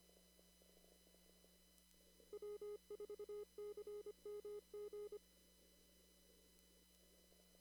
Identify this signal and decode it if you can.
Morse code, heard as a keyed tone, about 25 wpm, 420 Hz. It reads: W4CMG
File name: CW / kt5j wpx cw-op-k5pi